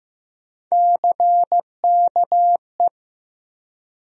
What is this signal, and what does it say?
Morse code, heard as a keyed tone, about 15 wpm, 700 Hz. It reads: CKE
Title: USA Call Sign Postfixes - 3 Letters 15wpm